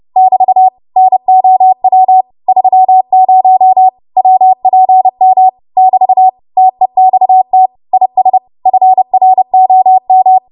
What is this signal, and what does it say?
Morse code, heard as a keyed tone, about 30 wpm, 750 Hz. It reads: <BT> NOW 30 WPM <BT> TEXT IS FROM